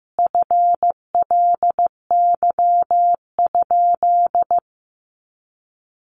Morse code, heard as a keyed tone, about 15 wpm, 700 Hz. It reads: FLY?
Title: Sentences from Top 500 Words 15wpm